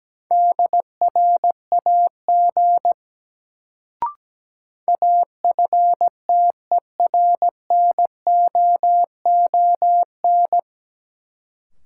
Morse code, heard as a keyed tone, about 17 wpm, 700 Hz. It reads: DRAG AFTERNOON
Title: Sentences from Top 1500 Words - New Word Drill - 17wpm